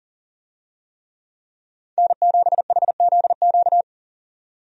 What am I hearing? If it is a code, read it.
Morse code, heard as a keyed tone, about 40 wpm, 700 Hz. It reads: N7HZQ
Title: USA Call Signs - 5 Characters - 40wpm